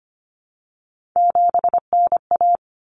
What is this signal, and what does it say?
Morse code, heard as a keyed tone, about 25 wpm, 700 Hz. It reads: 7NA
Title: Common Cut Numbers - 3-Digit 25wpm